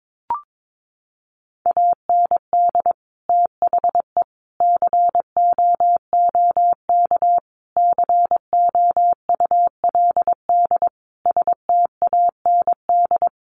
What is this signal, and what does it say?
Morse code, heard as a keyed tone, about 22 wpm, 700 Hz. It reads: AND THE COOK COULD STAND
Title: Sentences from Top 1300 Words 22wpm